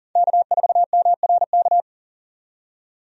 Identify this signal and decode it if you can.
Morse code, heard as a keyed tone, about 40 wpm, 700 Hz. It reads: K4MRK
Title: USA Call Signs - 5 Characters - 40wpm